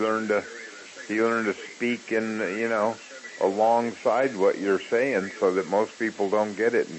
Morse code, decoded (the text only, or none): none